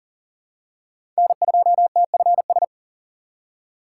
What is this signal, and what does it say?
Morse code, heard as a keyed tone, about 40 wpm, 700 Hz. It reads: N2TFS
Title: USA Call Signs - 5 Characters - 3x Character Spacing - 40wpm